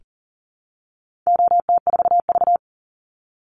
Morse code, heard as a keyed tone, about 37 wpm, 700 Hz. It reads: OT4V
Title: Call Signs - 40wpm